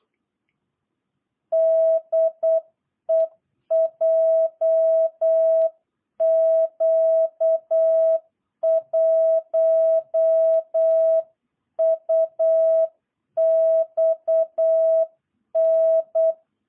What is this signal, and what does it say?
Morse code, heard as a keyed tone, about 8 wpm, 660 Hz. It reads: DE JQ1UXN
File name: jh8cbh / 2022cw